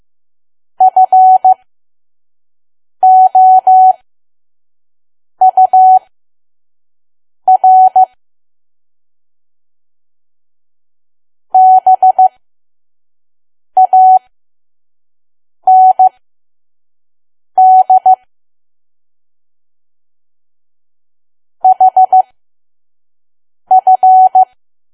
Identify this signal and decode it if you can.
Morse code, heard as a keyed tone, about 15 wpm, spaced as for 5 wpm, 750 Hz. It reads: FOUR BAND HF
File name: Archive / 5 WPM files / PM